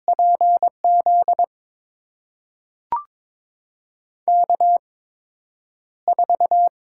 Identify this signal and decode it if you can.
Morse code, heard as a keyed tone, about 21 wpm, spaced as for 9 wpm, 700 Hz. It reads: PZ K4
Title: USA Call Signs - 5 Characters - 8x Character Spacing - 22wpm